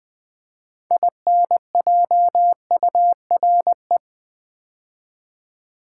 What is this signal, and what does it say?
Morse code, heard as a keyed tone, about 20 wpm, 700 Hz. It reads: INJURE
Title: Build A Word - 3 Letters At A Time - 6-Letter Words - 20wpm